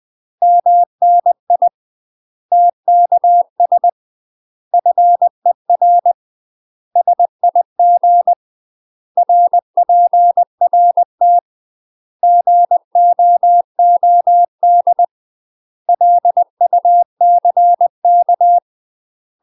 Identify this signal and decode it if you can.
Morse code, heard as a keyed tone, about 20 wpm, 700 Hz. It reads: MNI TKS FER SIG RPRT GOOD LUCK
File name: UK MASTER QSOs TRAINING / 20WPM / UK MASTER QSOs 0015